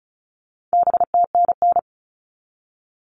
Morse code, heard as a keyed tone, about 34 wpm, 700 Hz. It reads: BTNN
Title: All Cut Numbers - 4-Digit 35wpm